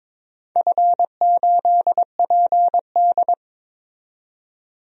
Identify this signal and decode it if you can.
Morse code, heard as a keyed tone, about 23 wpm, 700 Hz. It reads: F8PD
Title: DXCC-100 Call Signs - 4 Characters - 22wpm